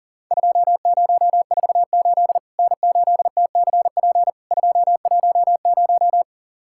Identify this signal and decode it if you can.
Morse code, heard as a keyed tone, about 40 wpm, 700 Hz. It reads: 2048 N8TCP 210
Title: Worked All Europe DX Contest - QTC Exchange - 40wpm